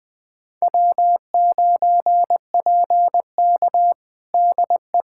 Morse code, heard as a keyed tone, about 20 wpm, 700 Hz. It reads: W9PK DE
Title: USA Callsign DE Callsign - 4 Characters - 20wpm